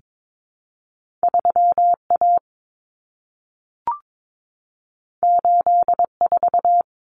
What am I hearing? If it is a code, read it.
Morse code, heard as a keyed tone, about 22 wpm, 700 Hz. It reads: 3A 84
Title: Common Cut Numbers - 2-Digit 22wpm